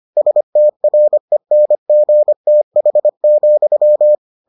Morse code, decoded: STRENGTH,

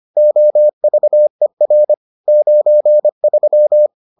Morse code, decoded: OVER 93

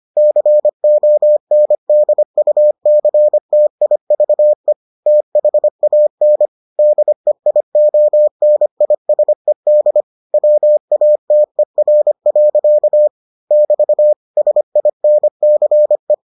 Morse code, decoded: CONDUCTIVE THAN DEIONISED WATER. <BT> SINCE

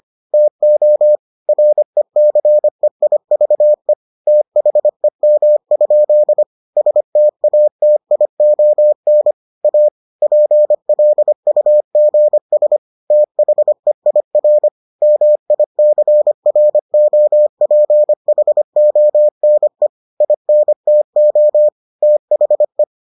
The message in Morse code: TO RECEIVE THEM? STATION A PLUGS THEIR MICROPHONE INTO THE

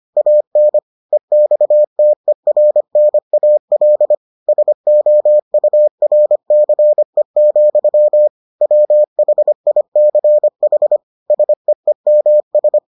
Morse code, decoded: AN EXTERNAL SOURCE, WHICH SEEMS

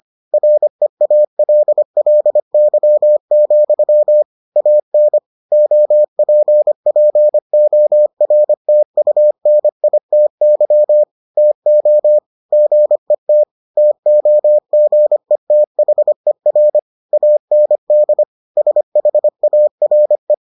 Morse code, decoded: REALLY, AN OPPORTUNITY TO GET TOGETHER AND SHARE

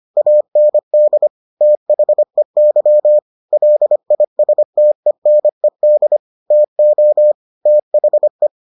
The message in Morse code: AND THEY LISTENED TO THE

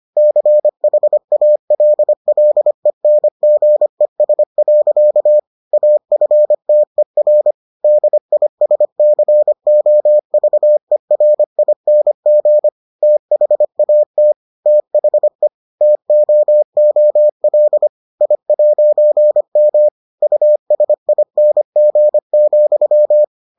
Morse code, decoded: CHALLENGES. AFTER DISCOVERING THAT THE TOOL I'M USING,

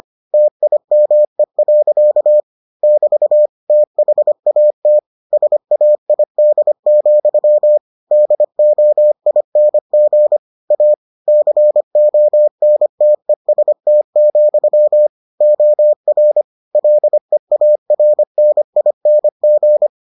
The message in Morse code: TIME. <BT> THAT SAID, DOING A CONTEST, OR LEARNING

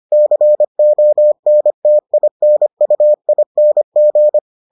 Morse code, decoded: CONTINUING

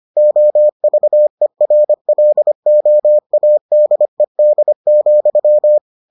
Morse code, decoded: OVERLOADED,

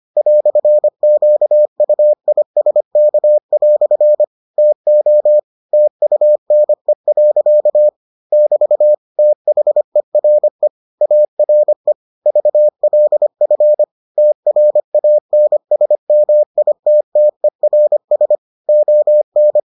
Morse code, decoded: "QUISK" TO TUNE. <BT> THERE ARE VLF TRANSMITTERS ON

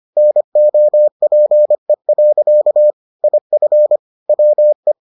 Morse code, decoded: NOPE. IF WE

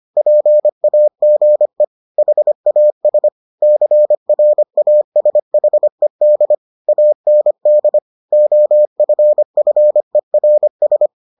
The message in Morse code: PAGE HAS CRASHED AND OFFERS

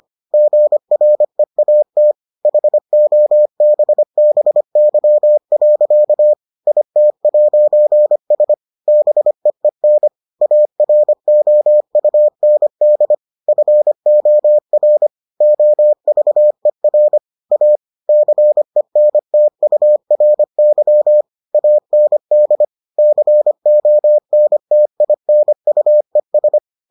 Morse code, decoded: GREAT HOBBY. IT'S BEEN AROUND FOR OVER A CENTURY AND CONTINUES

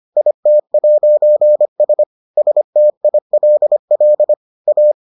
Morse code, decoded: IT'S STILL A

